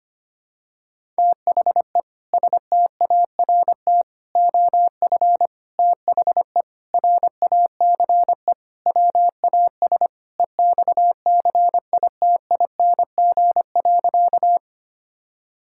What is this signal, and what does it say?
Morse code, heard as a keyed tone, about 25 wpm, 700 Hz. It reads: THE START OF THE RACE WAS EXCITING.
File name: practice / 25wpm / Sentences with 4-Word Phrases - 25wpm